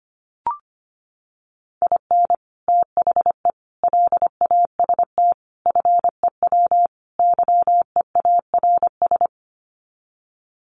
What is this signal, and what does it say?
Morse code, heard as a keyed tone, about 25 wpm, 700 Hz. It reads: IN THE LAST FEW YEARS
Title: Sets of 5 Words 25wpm